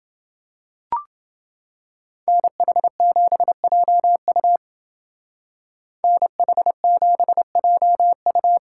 Morse code, heard as a keyed tone, about 28 wpm, 700 Hz. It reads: NH7JU NH7JU